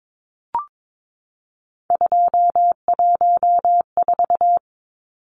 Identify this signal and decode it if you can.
Morse code, heard as a keyed tone, about 22 wpm, 700 Hz. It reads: 214
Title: Numbers - 3-Digit 22wpm